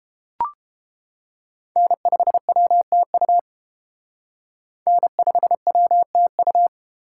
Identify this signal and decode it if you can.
Morse code, heard as a keyed tone, about 28 wpm, 700 Hz. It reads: N5WTU N5WTU